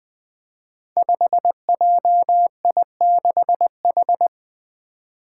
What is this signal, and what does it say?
Morse code, heard as a keyed tone, about 20 wpm, 700 Hz. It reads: 5JI6H
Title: Five Challenging Letter-Number Combinations 20wpm